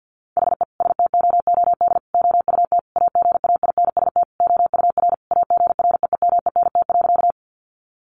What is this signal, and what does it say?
Morse code, heard as a keyed tone, about 48 wpm, 700 Hz. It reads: HE STOOD OUT AGAINST OUR AGREEMENT.
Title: Sentences from Top 1500 Words 50wpm